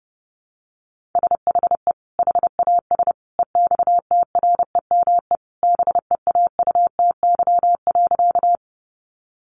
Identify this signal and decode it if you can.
Morse code, heard as a keyed tone, about 30 wpm, 700 Hz. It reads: SHE HAS EXTREME BEAUTY.
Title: Sentences from Top 1300 Words 30wpm